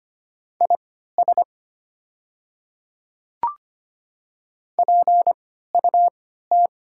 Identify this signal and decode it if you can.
Morse code, heard as a keyed tone, about 25 wpm, spaced as for 14 wpm, 700 Hz. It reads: IS PUT
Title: 3x Character Spacing - Top 300 Words - 25wpm